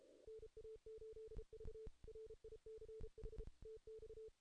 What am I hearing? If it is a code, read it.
Morse code, heard as a keyed tone, about 34 wpm, 440 Hz. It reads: NA8V RICH TX
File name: 2016 JAN CW NAQP / 2016 Jan naqp k5ot - 0-8